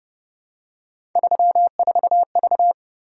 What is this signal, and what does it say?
Morse code, heard as a keyed tone, about 31 wpm, 700 Hz. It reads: 34V
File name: Speed-Racing - Three Letter-Number Combinations - 25wpm